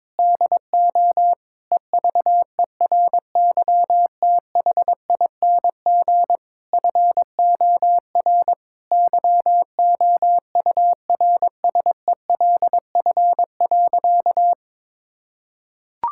Morse code, heard as a keyed tone, about 22 wpm, 700 Hz. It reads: DO EVERYTHING FOR YOURSELF.
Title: Sentences from Top 1000 Words - Review Part 6 - 22wpm